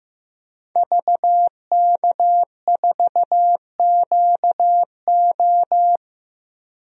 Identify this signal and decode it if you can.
Morse code, heard as a keyed tone, about 15 wpm, 700 Hz. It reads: VK4QO
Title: DXCC-100 Call Signs - 5 Characters - 15wpm